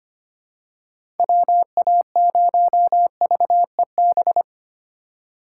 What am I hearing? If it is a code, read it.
Morse code, heard as a keyed tone, about 25 wpm, 700 Hz. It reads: WA0VEB
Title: USA Call Signs - 6 Characters - 2x Character Spacing - 25wpm